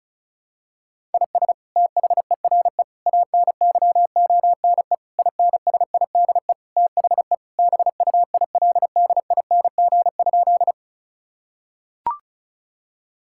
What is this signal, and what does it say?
Morse code, heard as a keyed tone, about 36 wpm, 700 Hz. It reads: IS THERE ANYONE INSIDE THE BUILDING?
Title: Sentences from Top 1000 Words - Review Part 6 - 35wpm